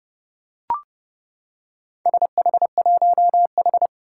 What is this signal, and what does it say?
Morse code, heard as a keyed tone, about 30 wpm, 690 Hz. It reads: SH1H
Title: Four Challenging Letter-Number Combinations 30wpm